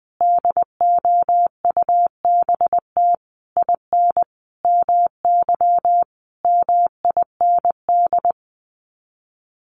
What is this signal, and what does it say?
Morse code, heard as a keyed tone, about 20 wpm, 700 Hz. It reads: DOUBT IN MY MIND
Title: Sets of 5 Words Encore2 20wpm